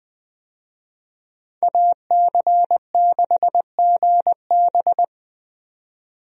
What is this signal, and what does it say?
Morse code, heard as a keyed tone, about 21 wpm, 700 Hz. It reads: AC6GB
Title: USA Call Signs - 5 Characters - 20wpm